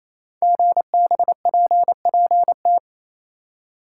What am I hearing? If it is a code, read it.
Morse code, heard as a keyed tone, about 28 wpm, 700 Hz. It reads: GBPPT
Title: Code Groups - 5 Letters - 28wpm